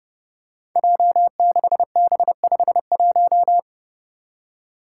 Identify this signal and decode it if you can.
Morse code, heard as a keyed tone, about 32 wpm, 700 Hz. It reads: J6B51
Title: Five Challenging Letter-Number Combinations 30wpm